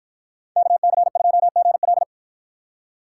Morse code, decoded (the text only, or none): KC2GL